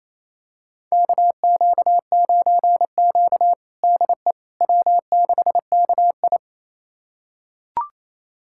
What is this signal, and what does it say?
Morse code, heard as a keyed tone, about 28 wpm, 700 Hz. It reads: KQ9Q DE W6KI
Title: USA Callsign DE Callsign - 4 Characters - 28wpm